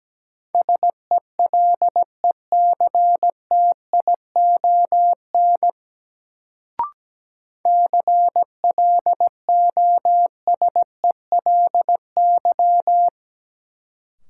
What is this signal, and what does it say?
Morse code, heard as a keyed tone, about 17 wpm, 700 Hz. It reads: SELECTION CLOSELY
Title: Sentences from Top 1400 Words - New Word Drill Alt - 17wpm